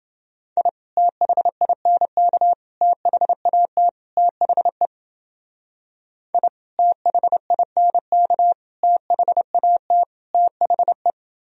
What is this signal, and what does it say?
Morse code, heard as a keyed tone, about 29 wpm, 700 Hz. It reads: I THINK THAT THE I THINK THAT THE